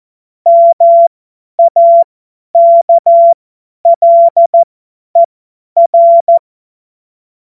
Morse code, heard as a keyed tone, about 14 wpm, spaced as for 10 wpm, 680 Hz. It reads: MAKLER
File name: Morsekurs / Witze / Witz 27 Februar 2024 50 F70